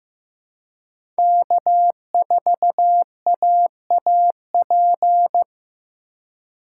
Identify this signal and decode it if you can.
Morse code, heard as a keyed tone, about 15 wpm, 700 Hz. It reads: K4AAP
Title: USA Call Signs - 5 Characters - 3x Character Spacing - 15wpm